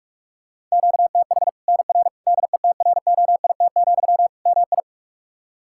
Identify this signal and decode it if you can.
Morse code, heard as a keyed tone, about 46 wpm, 700 Hz. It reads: QTH NR DETROIT, MI
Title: K1USN SST QSO Practice 45wpm